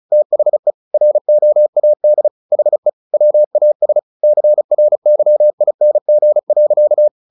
Morse code, heard as a keyed tone, about 36 wpm, 600 Hz. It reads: THE ROAD HE WAS CRYING.
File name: the old man and the sea 2 35wpm